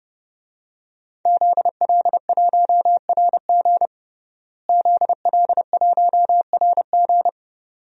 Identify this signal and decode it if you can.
Morse code, heard as a keyed tone, about 31 wpm, 700 Hz. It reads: ZL1RG ZL1RG
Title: DXCC-100 Call Signs - 5 Characters - Repeated 2x - 30wpm